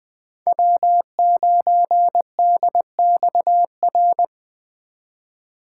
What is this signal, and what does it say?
Morse code, heard as a keyed tone, about 21 wpm, 700 Hz. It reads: W9DXR
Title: K1USN SST QSO Practice - 20wpm